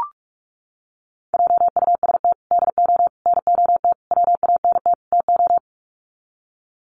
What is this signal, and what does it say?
Morse code, heard as a keyed tone, about 44 wpm, 700 Hz. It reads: JUST DO NOT WANT TO
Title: Sets of 5 Words Encore 45wpm